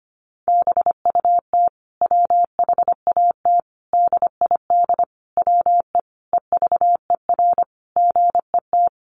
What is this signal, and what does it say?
Morse code, heard as a keyed tone, about 25 wpm, 700 Hz. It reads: BUT WHAT DID WE EVER GET
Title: Sentences from Top 400 Words 25wpm